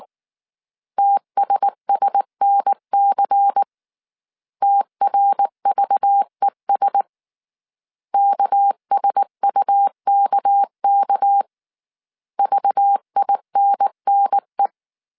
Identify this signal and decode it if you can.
Morse code, heard as a keyed tone, about 19 wpm, 800 Hz. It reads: TSSNC TRVES KSUKK VINNE
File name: CW / DET / 1 6 1500C S C 18wpm 2356